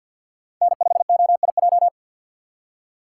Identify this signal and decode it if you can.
Morse code, heard as a keyed tone, about 60 wpm, 700 Hz. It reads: N5OIJ
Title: Speed Racing - USA 1x3 Call Signs 50wpm